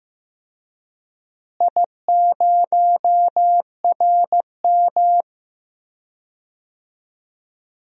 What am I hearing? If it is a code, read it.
Morse code, heard as a keyed tone, about 15 wpm, 700 Hz. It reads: I0RM
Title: DXCC-100 Call Signs - 4 Characters - Repeated 2x - 15wpm